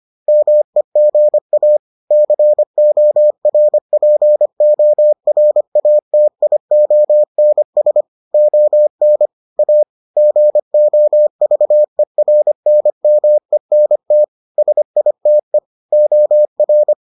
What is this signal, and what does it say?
Morse code, heard as a keyed tone, about 25 wpm, 600 Hz. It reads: MEGA CORPORATIONS ON A GOVERNMENT SITE OR